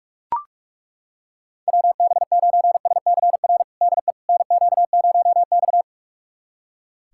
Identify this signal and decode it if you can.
Morse code, heard as a keyed tone, about 42 wpm, 700 Hz. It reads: WB9SCR DE NQ0X
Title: USA Callsign DE Callsign 45wpm